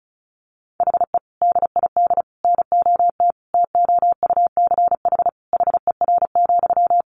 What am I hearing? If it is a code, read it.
Morse code, heard as a keyed tone, about 34 wpm, 700 Hz. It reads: HE DID NOT TOUCH HER,
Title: Sentences from Top 1500 Words 35wpm